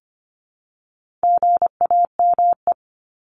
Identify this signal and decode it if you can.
Morse code, heard as a keyed tone, about 25 wpm, 700 Hz. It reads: GAME
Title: Top 500 Words Repeated 25wpm